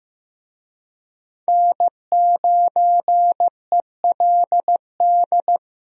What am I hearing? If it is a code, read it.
Morse code, heard as a keyed tone, about 15 wpm, 700 Hz. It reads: N9ELD
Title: USA Call Signs - 5 Characters - Repeated 2x - 15wpm